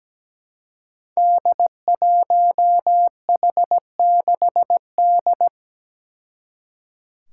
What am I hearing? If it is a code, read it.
Morse code, heard as a keyed tone, about 17 wpm, 700 Hz. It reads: D1H6D
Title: Five Challenging Letter-Number Combinations 17wpm